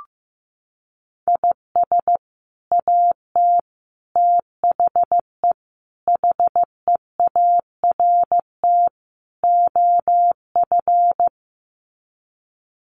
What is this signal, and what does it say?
Morse code, heard as a keyed tone, about 15 wpm, 700 Hz. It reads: IS AT THE HEART OF
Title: Sets of 5 Words Encore2 15wpm